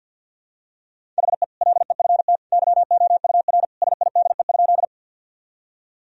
Morse code, heard as a keyed tone, about 49 wpm, 700 Hz. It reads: HE LEFT YOUR SIDE?
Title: Sentences from Top 1000 Words - Review Part 1 - 50wpm